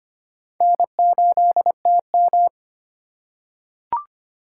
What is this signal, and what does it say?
Morse code, heard as a keyed tone, about 25 wpm, 700 Hz. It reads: N8TM
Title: USA Call Signs - 3x Character Spacing - 25wpm